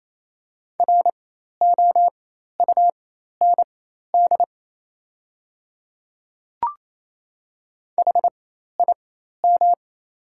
Morse code, heard as a keyed tone, about 28 wpm, spaced as for 13 wpm, 700 Hz. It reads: ROUND HIM
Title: 4x Character Spacing - Top 200 Words - 28wpm